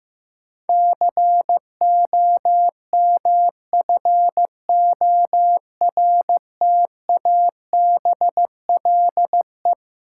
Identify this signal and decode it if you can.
Morse code, heard as a keyed tone, about 15 wpm, 700 Hz. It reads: COMFORTABLE
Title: Sentences from Top 1000 Words - Review Part 7 - New Word Drill - 15wpm